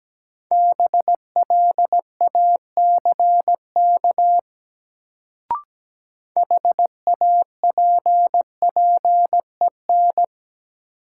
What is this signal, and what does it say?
Morse code, heard as a keyed tone, about 17 wpm, 700 Hz. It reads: BLACK HAPPEN
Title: Sentences from Top 400 Words - New Word Drill - 17wpm